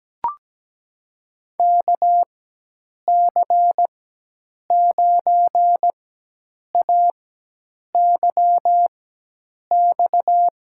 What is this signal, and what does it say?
Morse code, heard as a keyed tone, about 17 wpm, spaced as for 8 wpm, 700 Hz. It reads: KC9AYX
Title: USA Call Signs - 6 Characters - 4x Character Spacing - 17wpm